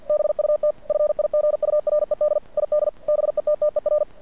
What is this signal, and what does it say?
Morse code, heard as a keyed tone, about 50 wpm, 590 Hz. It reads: BUT FIGURED ID BETTER
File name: archived / CRSnet